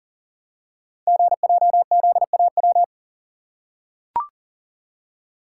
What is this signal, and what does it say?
Morse code, heard as a keyed tone, about 39 wpm, 690 Hz. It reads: GJZAW
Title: Code Groups - 5 Letters - 40wpm